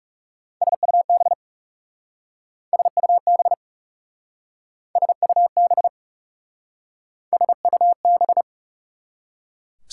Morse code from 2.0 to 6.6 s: SUB SUB

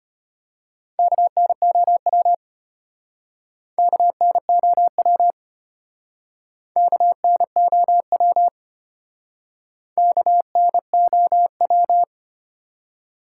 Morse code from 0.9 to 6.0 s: KNOW KNOW

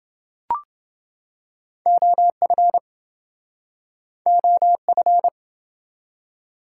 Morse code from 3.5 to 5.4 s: OF

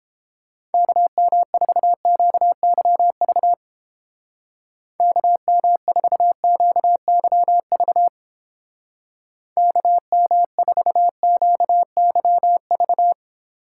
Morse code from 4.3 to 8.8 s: KM4QYV

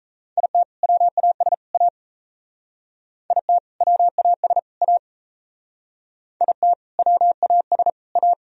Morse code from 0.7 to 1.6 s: WAS

